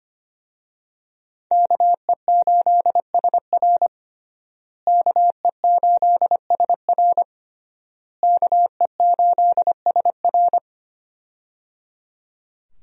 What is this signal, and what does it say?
Morse code, heard as a keyed tone, about 25 wpm, 700 Hz. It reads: KE8SR KE8SR KE8SR
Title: USA Call Signs - 5 Characters - Repeated 3x - 25wpm